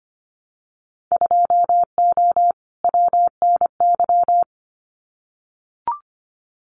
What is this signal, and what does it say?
Morse code, heard as a keyed tone, about 25 wpm, 700 Hz. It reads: 2O WNY
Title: Winter Field Day Exchanges Encore 25wpm